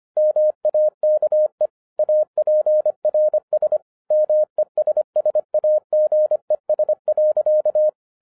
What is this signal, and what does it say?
Morse code, heard as a keyed tone, about 25 wpm, 610 Hz. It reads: MAKE APRS MESSAGES.
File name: 2011-10-mp3 / n09-CW TO PS2 CONVERTER 25wpm